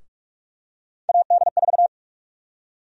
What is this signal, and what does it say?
Morse code, heard as a keyed tone, about 46 wpm, 700 Hz. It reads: AD4
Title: USA Call Sign Prefixes - 45wpm